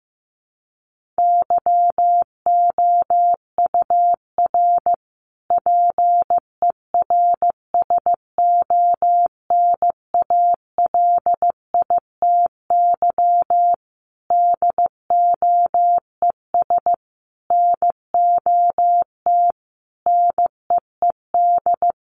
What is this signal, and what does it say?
Morse code, heard as a keyed tone, about 15 wpm, 700 Hz. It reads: YOUR PERSONALITY DOES NOT NEED
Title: Sentences from Top 1400 Words 15wpm